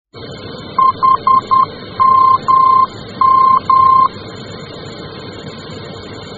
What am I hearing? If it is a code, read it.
Morse code, heard as a keyed tone, about 10 wpm, 1050 Hz. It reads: HMM
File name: germany / mx / HMM-D